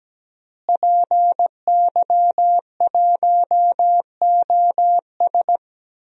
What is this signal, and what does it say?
Morse code, heard as a keyed tone, about 17 wpm, 700 Hz. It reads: PY1OS
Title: DXCC-100 Call Signs - 5 Characters - Repeated 3x - 17wpm